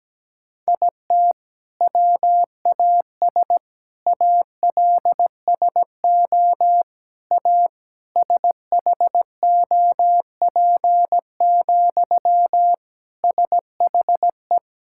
Morse code, heard as a keyed tone, about 17 wpm, 700 Hz. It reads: IT WAS ALSO A SHOP, SHE
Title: Sentences from Top 1600 Words 17wpm